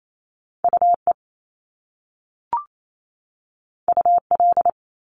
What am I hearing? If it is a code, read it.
Morse code, heard as a keyed tone, about 28 wpm, 700 Hz. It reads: UE UL
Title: Top Two Letter Combinations in Words 28wpm